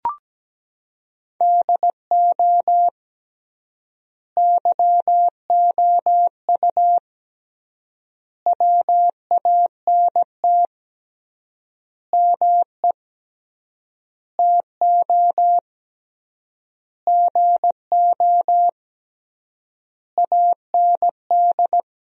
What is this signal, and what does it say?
Morse code, heard as a keyed tone, about 17 wpm, 700 Hz. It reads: DO YOU WANT ME TO GO AND
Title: 3x Word-Spacing - Sentences from Top 200 Words - 17wpm